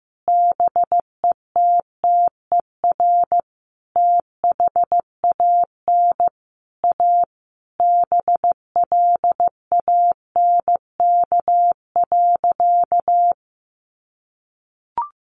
Morse code, heard as a keyed tone, about 14 wpm, 700 Hz. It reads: BETTER THAN A BLANK.
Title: Sentences from Top 1500 Words 15wpm